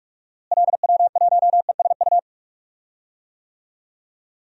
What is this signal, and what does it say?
Morse code, heard as a keyed tone, about 46 wpm, 700 Hz. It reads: LW1ESU
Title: DXCC-100 Call Signs - 6 Characters - Repeated 3x - 45wpm